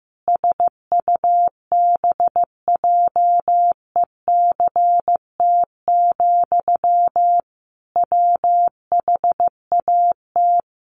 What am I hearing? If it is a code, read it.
Morse code, heard as a keyed tone, about 15 wpm, 700 Hz. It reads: SUBJECT, WHAT
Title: Sentences from Top 800 Words 15wpm